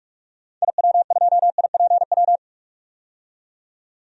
Morse code, heard as a keyed tone, about 43 wpm, 700 Hz. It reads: IW2IPW
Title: DXCC-100 Call Signs - 6 Characters - 45wpm